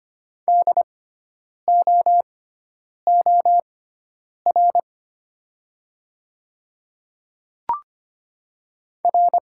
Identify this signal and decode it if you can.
Morse code, heard as a keyed tone, about 25 wpm, spaced as for 9 wpm, 700 Hz. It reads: DOOR R
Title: 6x Character Spacing - Top 500 Words - 25wpm